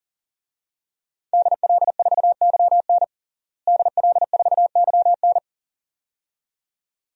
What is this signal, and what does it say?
Morse code, heard as a keyed tone, about 40 wpm, 700 Hz. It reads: DL4YN DL4YN
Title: DXCC-100 Call Signs - 5 Characters - Repeated 2x - 40wpm